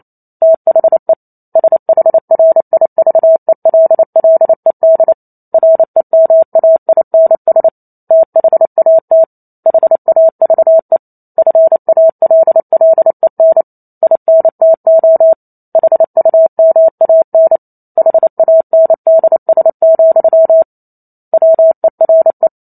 Morse code, heard as a keyed tone, about 28 wpm, 650 Hz. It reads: THE SHRIVELLED REMAINS THAT HAVE FALLEN INTO HUMAN HANDS, WERE